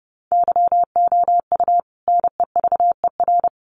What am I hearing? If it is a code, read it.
Morse code, heard as a keyed tone, about 30 wpm, 700 Hz. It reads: YOU NEVER
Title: Sentences from Top 1500 Words 30wpm